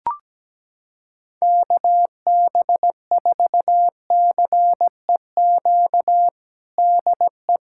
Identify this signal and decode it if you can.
Morse code, heard as a keyed tone, about 17 wpm, 700 Hz. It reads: KB4CEQ DE
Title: USA Callsign DE Callsign 17wpm